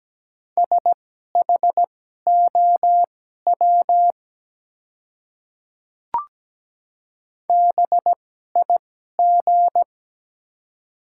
Morse code, heard as a keyed tone, about 17 wpm, spaced as for 12 wpm, 700 Hz. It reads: SHOW BIG
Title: 2x Character Spacing - Top 300 Words - 17wpm